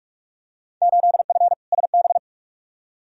Morse code, heard as a keyed tone, about 42 wpm, 700 Hz. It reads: 8F SB
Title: Field Day Exchanges Encore - 45wpm